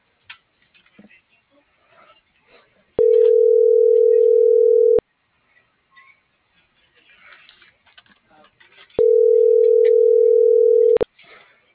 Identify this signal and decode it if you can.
Ringback tone (North America), 440 + 480 Hz.
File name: CENTRAL MABE / out noemi.gutierrez